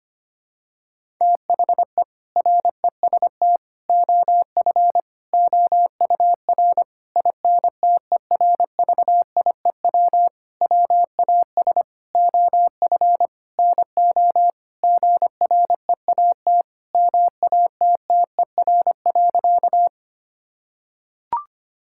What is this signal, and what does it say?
Morse code, heard as a keyed tone, about 25 wpm, 700 Hz. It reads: THE REST OF OUR INTERVIEW WAS OF NO GREAT MATTER.
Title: Sentences from Top 1600 Words 25wpm